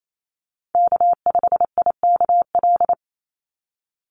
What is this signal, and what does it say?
Morse code, heard as a keyed tone, about 28 wpm, 700 Hz. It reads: K5IKL
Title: USA Call Signs 28wpm